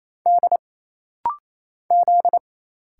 Morse code, heard as a keyed tone, about 27 wpm, 700 Hz. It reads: D Z
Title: Single Letters - Rapid-Fire 28wpm